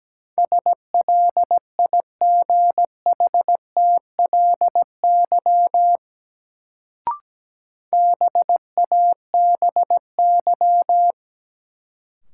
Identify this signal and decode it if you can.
Morse code, heard as a keyed tone, about 17 wpm, 700 Hz. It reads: SLIGHTLY BABY
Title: Sentences from Top 1300 Words - New Word Drill Alt - 17wpm